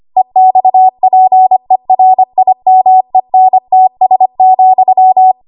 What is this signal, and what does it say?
Morse code, heard as a keyed tone, about 25 wpm, 750 Hz. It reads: EXPERIMENTS,
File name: Archive / 25 WPM / 25WPM